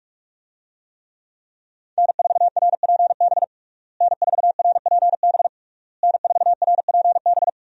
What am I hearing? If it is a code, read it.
Morse code, heard as a keyed tone, about 46 wpm, 700 Hz. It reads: N4RPB N4RPB N4RPB
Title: USA Call Signs - 5 Characters - Repeated 3x - 45wpm